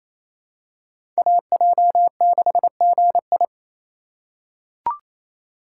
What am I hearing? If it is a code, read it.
Morse code, heard as a keyed tone, about 27 wpm, 700 Hz. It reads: AJ6GI
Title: USA Call Signs - 4x Character Spacing - 28wpm